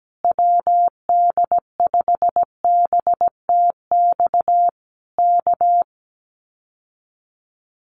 Morse code, heard as a keyed tone, about 17 wpm, 700 Hz. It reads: WD5BTX K
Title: Common QSO Phrases Encore 17wpm